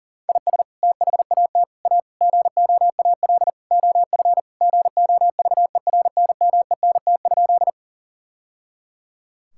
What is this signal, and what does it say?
Morse code, heard as a keyed tone, about 40 wpm, 700 Hz. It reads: IS THAT A GOAL OF GOVERNMENT?
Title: Sentences from Top 1000 Words - Review Part 5 - 40wpm